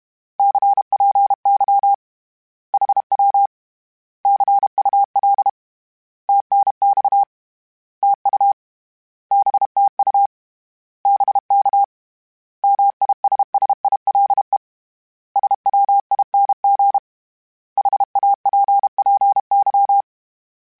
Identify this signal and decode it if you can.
Morse code, heard as a keyed tone, about 31 wpm, 800 Hz. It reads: CPY HW CUL TNX TU BTU BK MISSILE SWING HAPPY
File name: Short Runs / Short Runs-mp3s / Words @30